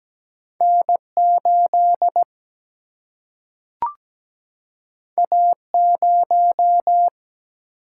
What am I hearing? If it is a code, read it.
Morse code, heard as a keyed tone, about 17 wpm, 700 Hz. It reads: N 8 A 0
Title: Single-Letter Single-Number 17wpm